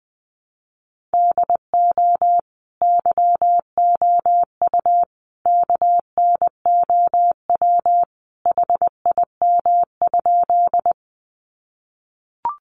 Morse code, heard as a keyed tone, about 20 wpm, 700 Hz. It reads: DO YOU KNOW HIM?
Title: Sentences from Top 100 Words 20wpm